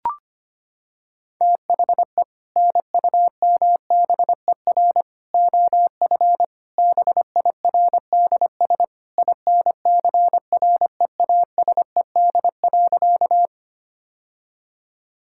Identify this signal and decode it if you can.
Morse code, heard as a keyed tone, about 25 wpm, 700 Hz. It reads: THE NUMBER OF BIRDS INCREASED.
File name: practice / 25wpm / Sentences with 3-Word Phrases - 25wpm